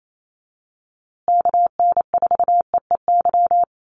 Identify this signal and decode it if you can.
Morse code, heard as a keyed tone, about 28 wpm, 700 Hz. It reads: KN4EEY
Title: USA Call Signs Repeated 3x 28wpm